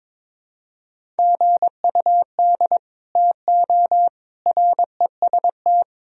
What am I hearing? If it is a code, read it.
Morse code, heard as a keyed tone, about 22 wpm, 700 Hz. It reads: GUD TO REST
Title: Sentences from Top 500 Words and Abbreviations 22wpm